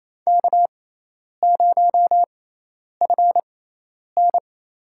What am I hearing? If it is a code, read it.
Morse code, heard as a keyed tone, about 28 wpm, spaced as for 10 wpm, 700 Hz. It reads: K0FN
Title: USA Call Signs - 6x Character Spacing - 28wpm